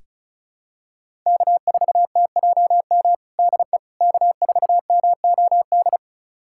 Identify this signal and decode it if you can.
Morse code, heard as a keyed tone, about 36 wpm, 700 Hz. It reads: K4TJM DE K4MOD
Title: USA Callsign DE Callsign - 5 Characters - 35wpm